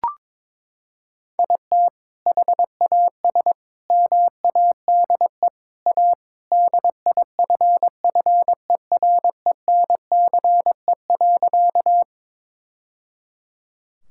Morse code, heard as a keyed tone, about 22 wpm, 700 Hz. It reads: IT HAS MADE A DIFFERENCE.
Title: Sentences from Top 1000 Words - Review Part 5 - 22wpm